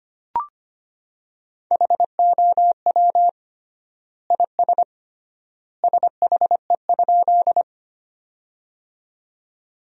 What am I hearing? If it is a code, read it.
Morse code, heard as a keyed tone, about 25 wpm, 700 Hz. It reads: HOW IS SHE?
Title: 3x Word-Spacing - Sentences from Top 100 Words - 25wpm